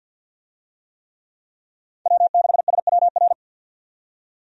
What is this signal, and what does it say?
Morse code, heard as a keyed tone, about 52 wpm, 700 Hz. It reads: W6SWR
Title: USA Call Signs - 5 Characters - 50wpm